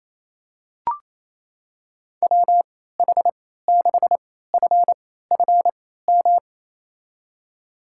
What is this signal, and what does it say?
Morse code, heard as a keyed tone, about 27 wpm, spaced as for 16 wpm, 690 Hz. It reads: WH6FFM
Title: USA Call Signs - 3x Character Spacing - 28wpm